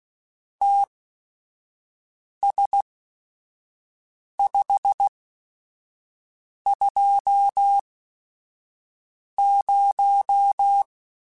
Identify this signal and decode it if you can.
Morse code, heard as a keyed tone, about 16 wpm, spaced as for 5 wpm, 800 Hz. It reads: TS520
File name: QSOs / QSOs-mp3s / WD0FBT 1@5